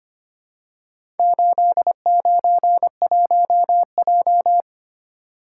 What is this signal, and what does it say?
Morse code, heard as a keyed tone, about 25 wpm, 700 Hz. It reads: 891J
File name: practice / 25wpm / Four Letter-Number Combinations - 25wpm